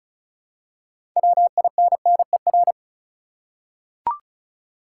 Morse code, heard as a keyed tone, about 34 wpm, 700 Hz. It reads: WINNER
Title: Six Letter Words - 35wpm